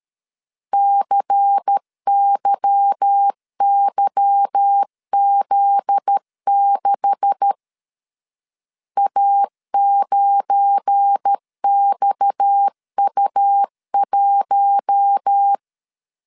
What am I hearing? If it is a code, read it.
Morse code, heard as a keyed tone, about 13 wpm, 800 Hz. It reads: CYYZ6 A9XU1
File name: CW / DET / 1 9 1500C S A 12wpm 3800